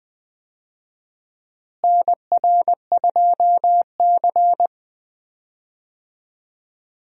Morse code, heard as a keyed tone, about 20 wpm, 700 Hz. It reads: NR2C
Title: USA Call Signs - 4 Characters - Repeated 3x - 20wpm